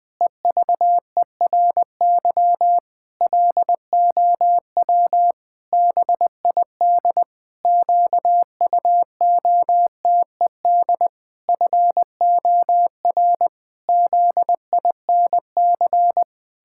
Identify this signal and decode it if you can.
Morse code, heard as a keyed tone, about 20 wpm, 700 Hz. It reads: EVERY LOW BID QUOTED FOR ZINC
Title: Pangram Challenge v2 - 20wpm